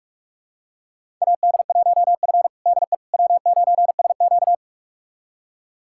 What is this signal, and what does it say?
Morse code, heard as a keyed tone, about 41 wpm, 700 Hz. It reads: AD1F DE W9SQ
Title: USA Callsign DE Callsign - 4 Characters - 45wpm